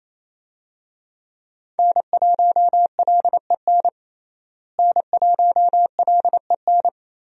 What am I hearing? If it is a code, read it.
Morse code, heard as a keyed tone, about 28 wpm, 700 Hz. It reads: N1LEN N1LEN
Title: USA Call Signs - 5 Characters - Repeated 3x - 28wpm